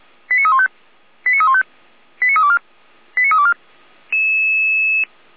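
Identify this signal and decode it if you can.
Five-tone selective call (ZVEI), heard as quick runs of five tones, tones of about 70 ms. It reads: 89315 89315 89324 89324
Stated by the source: ZVEI five-tone selective call received on SDR.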